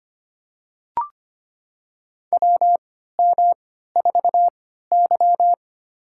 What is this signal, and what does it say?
Morse code, heard as a keyed tone, about 25 wpm, spaced as for 14 wpm, 700 Hz. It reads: WM4Y
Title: USA Call Signs - 3x Character Spacing - 25wpm